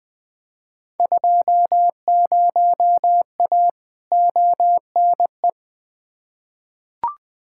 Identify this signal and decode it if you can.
Morse code, heard as a keyed tone, about 20 wpm, 700 Hz. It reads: 20A ONE
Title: Field Day Exchanges Encore - 20wpm